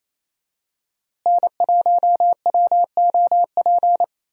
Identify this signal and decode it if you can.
Morse code, heard as a keyed tone, about 28 wpm, 700 Hz. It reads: N1WOP
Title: USA Call Signs - 5 Characters - Repeated 3x - 28wpm